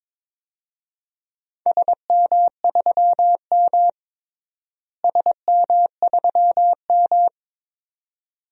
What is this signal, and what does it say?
Morse code, heard as a keyed tone, about 21 wpm, 700 Hz. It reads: SM3M SM3M
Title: DXCC-100 Call Signs - 4 Characters - Repeated 2x - 22wpm